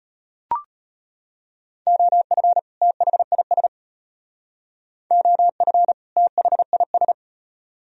Morse code, heard as a keyed tone, about 32 wpm, 700 Hz. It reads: OF THIS OF THIS